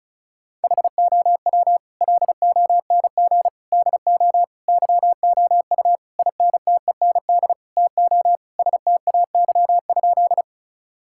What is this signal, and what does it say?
Morse code, heard as a keyed tone, about 34 wpm, 700 Hz. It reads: HOW LONG DO YOU INTEND TO STAY?
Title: Sentences from Top 1600 Words 35wpm